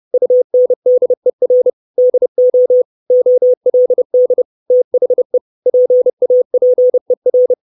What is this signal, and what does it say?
Morse code, heard as a keyed tone, about 30 wpm, 500 Hz. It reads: UNDER DO OLD THE PAPER
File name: 632 Words 30WPM